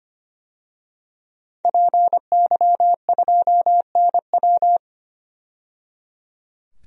Morse code, heard as a keyed tone, about 25 wpm, 700 Hz. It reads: PY2NW
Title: DXCC-100 Call Signs - 5 Characters - 25wpm